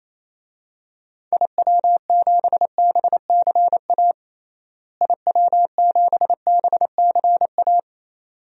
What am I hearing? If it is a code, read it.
Morse code, heard as a keyed tone, about 27 wpm, 700 Hz. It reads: IW7BCA IW7BCA
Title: DXCC-100 Call Signs - 6 Characters - Repeated 2x - 28wpm